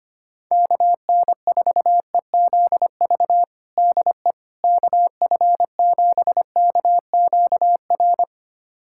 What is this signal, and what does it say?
Morse code, heard as a keyed tone, about 25 wpm, 700 Hz. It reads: KN4EZV DE KF7KQR
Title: USA Callsign DE Callsign 25wpm